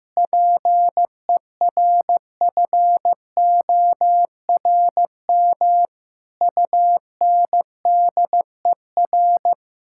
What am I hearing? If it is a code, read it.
Morse code, heard as a keyed tone, about 15 wpm, 700 Hz. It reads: PERFORM UNDER
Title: Sentences from Top 1000 Words - Review Part 6 - 15wpm